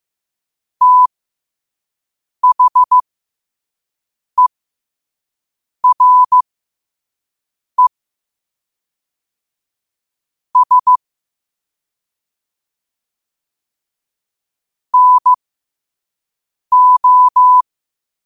Morse code, heard as a keyed tone, about 15 wpm, spaced as for 5 wpm, 1000 Hz. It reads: THERE S NO
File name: qotdfiles / 5wpm